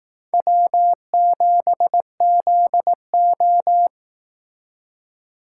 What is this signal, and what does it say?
Morse code, heard as a keyed tone, about 19 wpm, 700 Hz. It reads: W7ZO